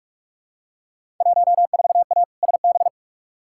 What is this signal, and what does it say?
Morse code, heard as a keyed tone, about 46 wpm, 700 Hz. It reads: 14A SB
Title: Field Day Exchanges Encore - 45wpm